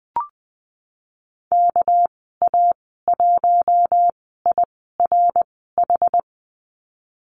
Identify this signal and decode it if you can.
Morse code, heard as a keyed tone, about 20 wpm, spaced as for 14 wpm, 700 Hz. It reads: KA1IRH
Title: 200 Lesson 32g - Letter k - Call Signs 20wpm